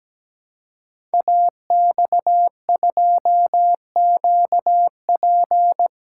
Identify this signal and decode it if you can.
Morse code, heard as a keyed tone, about 17 wpm, 700 Hz. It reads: AX2QP
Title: Code Groups - 5 Letter:Numbers - 17wpm